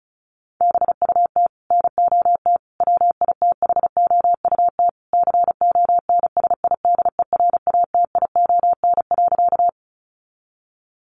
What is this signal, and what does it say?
Morse code, heard as a keyed tone, about 34 wpm, 700 Hz. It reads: BUT NOT WITHOUT CONSIDERATION.
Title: Sentences from Top 1500 Words 35wpm